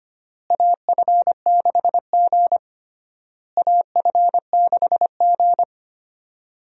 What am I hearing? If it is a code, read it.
Morse code, heard as a keyed tone, about 25 wpm, 700 Hz. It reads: AF6G AF6G
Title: USA Call Signs - 4 Characters - Repeated 3x - 25wpm